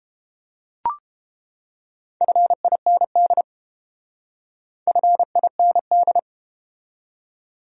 Morse code, heard as a keyed tone, about 32 wpm, 700 Hz. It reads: FIND FIND